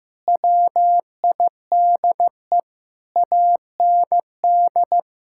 Morse code, heard as a keyed tone, about 15 wpm, 700 Hz. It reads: WIDE AND
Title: Sentences from Top 1000 Words - Review Part 6 - 15wpm